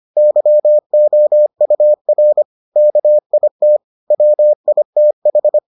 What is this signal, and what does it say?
Morse code, heard as a keyed tone, about 25 wpm, 600 Hz. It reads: YOUR KIT WITH